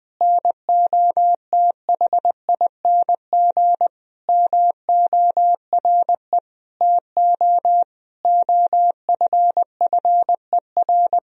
Morse code, heard as a keyed tone, about 20 wpm, 700 Hz. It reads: NOTHING MORE TO OFFER
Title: Sentences from Top 1000 Words - Review Part 4 - 20wpm